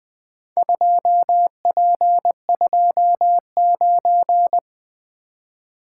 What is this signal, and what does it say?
Morse code, heard as a keyed tone, about 20 wpm, 700 Hz. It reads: 2P29
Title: Four Letter-Number Combinations - 20wpm